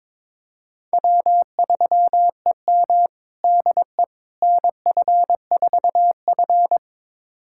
Morse code, heard as a keyed tone, about 22 wpm, 700 Hz. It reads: W3EM DE NF4F
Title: USA Callsign DE Callsign - 4 Characters - 22wpm